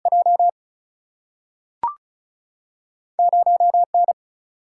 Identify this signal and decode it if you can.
Morse code, heard as a keyed tone, about 36 wpm, 700 Hz. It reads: J 0 N
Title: Single-Letter Single-Number 35wpm